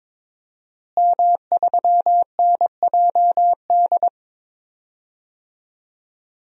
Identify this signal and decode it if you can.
Morse code, heard as a keyed tone, about 21 wpm, 700 Hz. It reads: M3NJD
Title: DXCC-100 Call Signs - 5 Characters - Repeated 2x - 22wpm